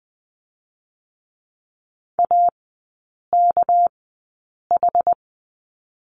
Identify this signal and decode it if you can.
Morse code, heard as a keyed tone, about 20 wpm, spaced as for 8 wpm, 690 Hz. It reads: AKH
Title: 198 Lesson 32e - Letter k - Three Characters 20wpm